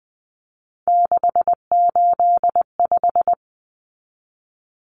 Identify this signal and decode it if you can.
Morse code, heard as a keyed tone, about 20 wpm, 700 Hz. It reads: 685
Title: Common Cut Numbers - 3-Digit 20wpm